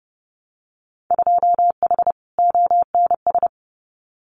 Morse code, heard as a keyed tone, about 30 wpm, 700 Hz. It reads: 2H ONS
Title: Winter Field Day Exchanges Encore 30wpm